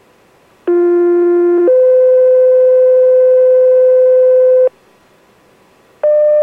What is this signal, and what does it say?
Two-tone page, heard as a one-second tone, then a three-second tone, about 350 then 510 Hz. It is heard from 0.7 to 4.7 s.